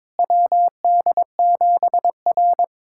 Morse code, heard as a keyed tone, about 22 wpm, 700 Hz. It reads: WD7R
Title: USA Callsign DE Callsign - 4 Characters - 22wpm